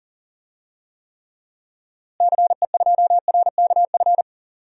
Morse code, heard as a keyed tone, about 40 wpm, 700 Hz. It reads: CE2RKF
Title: DXCC-100 Call Signs - 6 Characters - Repeated 2x - 40wpm